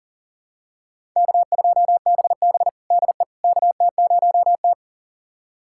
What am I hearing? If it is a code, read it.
Morse code, heard as a keyed tone, about 40 wpm, 700 Hz. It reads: K2BB DE KT0T
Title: USA Callsign DE Callsign - 4 Characters - 40wpm